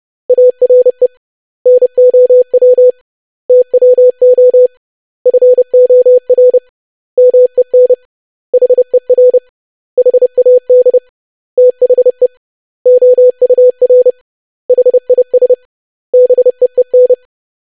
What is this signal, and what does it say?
Morse code, heard as a keyed tone, about 30 wpm, 500 Hz. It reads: ARE NOW TWO FOR MEN HER HAD THE OUR HIS BEEN
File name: lesson / web / basic100-25wpm